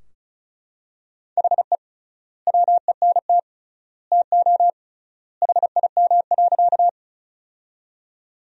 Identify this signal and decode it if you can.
Morse code, heard as a keyed tone, about 34 wpm, 700 Hz. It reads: HE WENT TO HIM.
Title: 3x Word-Spacing - Sentences from Top 200 Words - 35wpm